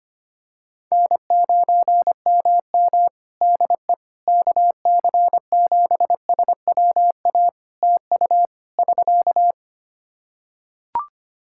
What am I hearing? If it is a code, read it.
Morse code, heard as a keyed tone, about 25 wpm, 700 Hz. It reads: N9MM DE KC7SWA TU <SK>
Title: Practice QSO - Exchange 4 - Ending - 25wpm